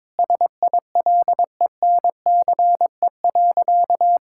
Morse code, heard as a keyed tone, about 21 wpm, 700 Hz. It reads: SILENCE.
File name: practice / 22wpm / Sentences with 3-Word Phrases - 22wpm